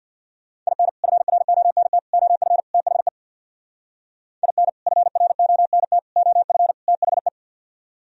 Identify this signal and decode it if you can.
Morse code, heard as a keyed tone, about 62 wpm, 700 Hz. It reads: IN FRONT OF THE IN FRONT OF THE
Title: Speed-Racing - Sets of 4 Words 50wpm